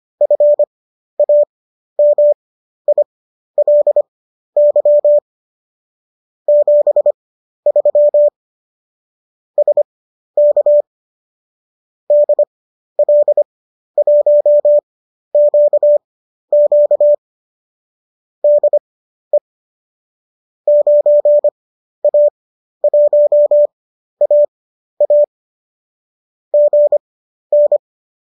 Morse code, heard as a keyed tone, about 25 wpm, spaced as for 12 wpm, 590 Hz. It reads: FAMILY 73 SK DL1QQ DE 9A1AA GN
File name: qso012 12